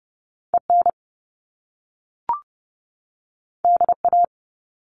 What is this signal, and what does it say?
Morse code, heard as a keyed tone, about 30 wpm, 700 Hz. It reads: EN DA
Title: Top Two Letter Combinations in Words 30wpm